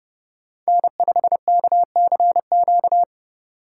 Morse code, heard as a keyed tone, about 31 wpm, 700 Hz. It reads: N5KCQ
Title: USA Call Signs - 5 Characters - 4x Character Spacing - 30wpm